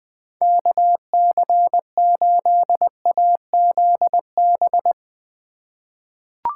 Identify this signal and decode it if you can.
Morse code, heard as a keyed tone, about 20 wpm, 700 Hz. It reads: KC8AZB
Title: USA Call Signs - 6 Characters - 2x Character Spacing - 20wpm